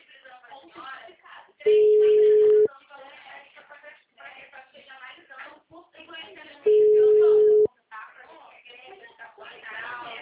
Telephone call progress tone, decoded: ringback tone

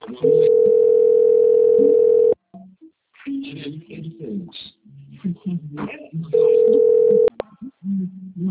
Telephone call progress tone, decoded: ringback tone

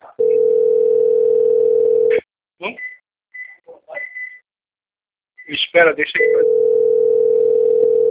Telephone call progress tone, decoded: ringback tone

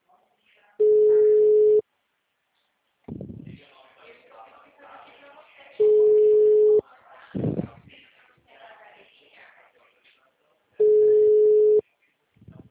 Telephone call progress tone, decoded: ringback tone

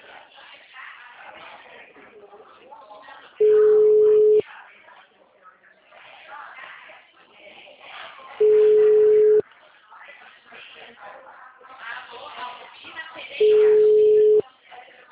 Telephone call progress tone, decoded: ringback tone

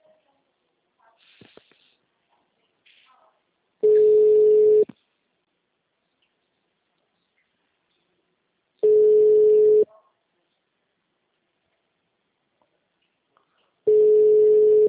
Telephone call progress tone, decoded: ringback tone

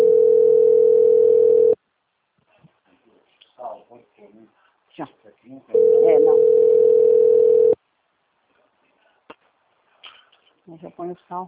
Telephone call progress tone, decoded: ringback tone

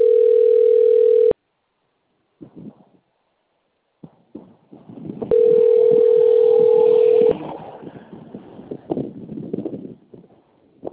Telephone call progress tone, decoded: ringback tone

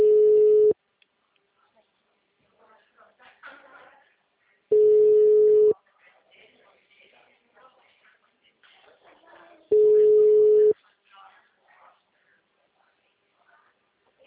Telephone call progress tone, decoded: ringback tone

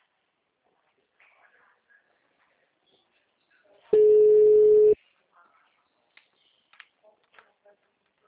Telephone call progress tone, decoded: ringback tone

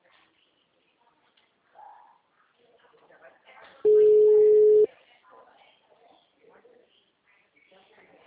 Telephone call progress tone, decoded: ringback tone

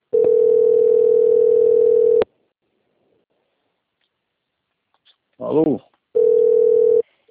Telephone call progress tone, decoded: ringback tone

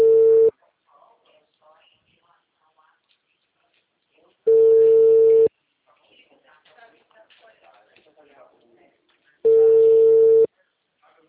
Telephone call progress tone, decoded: ringback tone

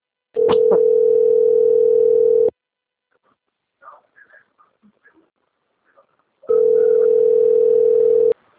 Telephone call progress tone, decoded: ringback tone

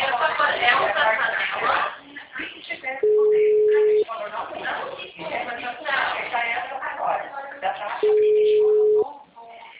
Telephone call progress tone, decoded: ringback tone

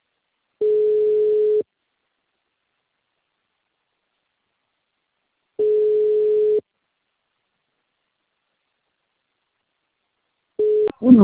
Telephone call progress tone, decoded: ringback tone

